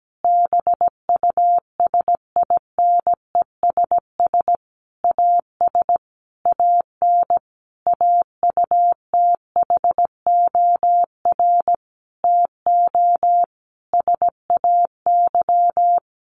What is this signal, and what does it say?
Morse code, heard as a keyed tone, about 17 wpm, 700 Hz. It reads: BUSINESS AS AN AUTHOR TO SAY
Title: Sentences from Top 1200 Words 17wpm